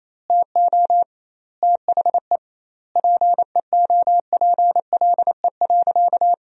Morse code, heard as a keyed tone, about 28 wpm, 700 Hz. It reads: TO THE PEOPLE.
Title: 2x Word-Spacing - Sentences from Top 400 Words - 28wpm